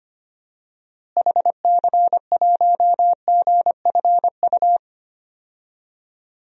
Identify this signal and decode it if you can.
Morse code, heard as a keyed tone, about 25 wpm, 690 Hz. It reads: HC1GFU
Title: DXCC-100 Call Signs - 6 Characters - Repeated 3x - 25wpm